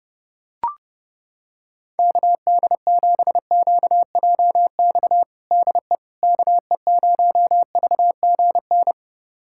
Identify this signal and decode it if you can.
Morse code, heard as a keyed tone, about 32 wpm, 700 Hz. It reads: KD7QJX DE KE0VGN
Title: USA Callsign DE Callsign 30wpm